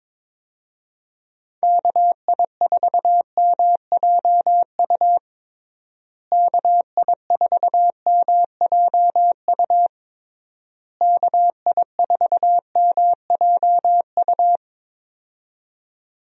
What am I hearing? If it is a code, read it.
Morse code, heard as a keyed tone, about 21 wpm, 700 Hz. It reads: KI4MJU KI4MJU KI4MJU
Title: USA Call Signs - 6 Characters - Repeated 3x - 22wpm